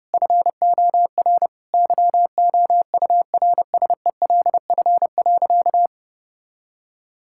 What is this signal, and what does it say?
Morse code, heard as a keyed tone, about 32 wpm, 700 Hz. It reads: FOR YOURSELF.
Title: Sentences from Top 1000 Words - Review Part 6 - 30wpm